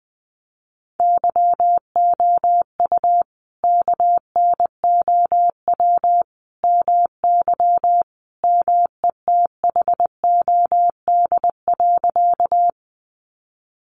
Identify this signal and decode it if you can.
Morse code, heard as a keyed tone, about 20 wpm, 700 Hz. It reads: YOU KNOW MY METHOD.
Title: Sentences from Top 600 Words 20wpm